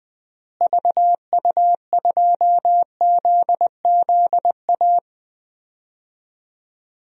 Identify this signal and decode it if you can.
Morse code, heard as a keyed tone, about 21 wpm, 700 Hz. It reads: VU2ZZA
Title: DXCC-100 Call Signs - 6 Characters - Repeated 3x - 20wpm